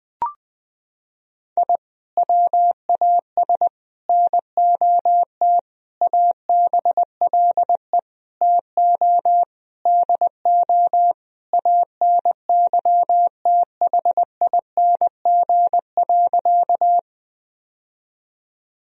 Morse code, heard as a keyed tone, about 20 wpm, 700 Hz. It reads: I WAS NOT ABLE TO DO ANYTHING.
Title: Sentences from Top 1000 Words - Review Part 2 - 20wpm